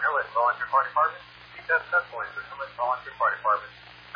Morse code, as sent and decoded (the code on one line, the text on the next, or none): none